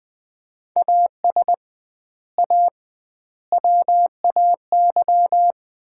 Morse code, heard as a keyed tone, about 20 wpm, 700 Hz. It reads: AS A WAY
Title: 2x Word-Spacing - Sets of 4 Words - 20wpm